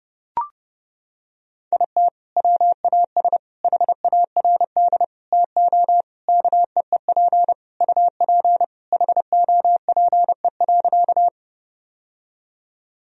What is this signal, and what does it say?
Morse code, heard as a keyed tone, about 30 wpm, 700 Hz. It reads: IT WAS HARD TO KEEP UP HOPE.
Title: Sentences from Top 1000 Words - Review Part 1 - 30wpm